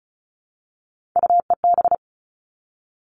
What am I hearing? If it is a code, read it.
Morse code, heard as a keyed tone, about 34 wpm, 700 Hz. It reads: UEB
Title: All Cut Numbers - 3-Digit 35wpm